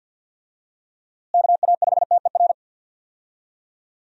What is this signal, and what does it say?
Morse code, heard as a keyed tone, about 51 wpm, 700 Hz. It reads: KA5TER
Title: USA Call Signs - 6 Characters - Repeated 3x - 50wpm